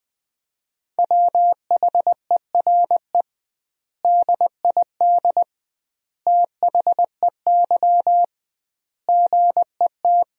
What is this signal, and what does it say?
Morse code, heard as a keyed tone, about 20 wpm, 700 Hz. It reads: WHERE DID THEY GET
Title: 2x Word-Spacing - Sentences from Top 400 Words - 20wpm